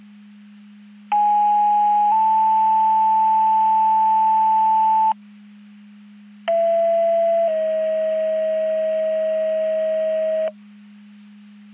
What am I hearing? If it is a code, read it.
Two-tone page, heard as a one-second tone, then a three-second tone, about 830 then 880 Hz, about 670 then 640 Hz. It is heard from 1.1 to 10.5 s.
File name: Portage 154.385 Paging